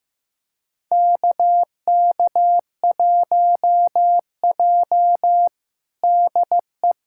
Morse code, heard as a keyed tone, about 15 wpm, 700 Hz. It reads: KK1J DE
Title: USA Callsign DE Callsign - 4 Characters - 15wpm